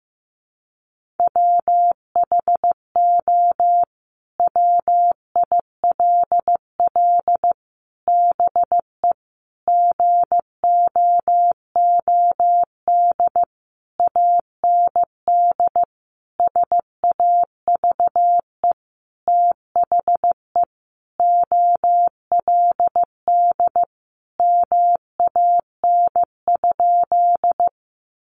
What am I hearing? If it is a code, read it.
Morse code, heard as a keyed tone, about 15 wpm, 700 Hz. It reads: WHO WILL BE GOOD AND SAVE THE OLD MAN?
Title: Sentences from Top 1100 Words 15wpm